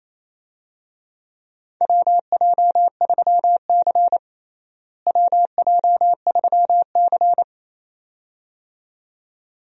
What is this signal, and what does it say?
Morse code, heard as a keyed tone, about 28 wpm, 700 Hz. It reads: WJ3C WJ3C
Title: USA Call Signs - 4 Characters - Repeated 2x - 28wpm